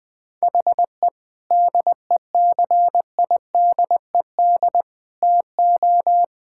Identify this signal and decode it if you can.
Morse code, heard as a keyed tone, about 21 wpm, 700 Hz. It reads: HE DECIDED TO
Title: Sentences with 7-Word Phrases - 20wpm